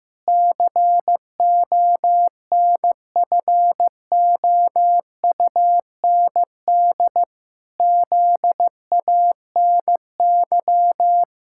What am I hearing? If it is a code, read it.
Morse code, heard as a keyed tone, about 15 wpm, 700 Hz. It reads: CONFOUND ZANY
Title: Pangram Challenge v2 - 15wpm